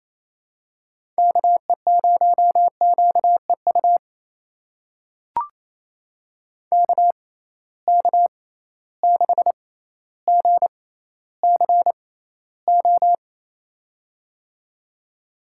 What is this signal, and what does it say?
Morse code, heard as a keyed tone, about 27 wpm, spaced as for 14 wpm, 700 Hz. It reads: KE0QEU KK6GCO
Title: USA Call Signs - 6 Characters - 6x Character Spacing - 28wpm